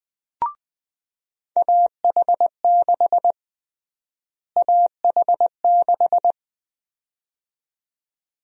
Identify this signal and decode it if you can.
Morse code, heard as a keyed tone, about 21 wpm, 700 Hz. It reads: AH6 AH6
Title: USA Call Sign Prefixes - Repeated 2x - 20wpm